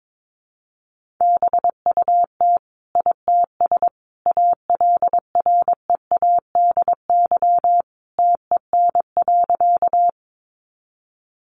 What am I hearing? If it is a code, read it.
Morse code, heard as a keyed tone, about 22 wpm, 700 Hz. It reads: BUT ITS ALREADY TEN.
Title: Sentences from Top 700 Words 22wpm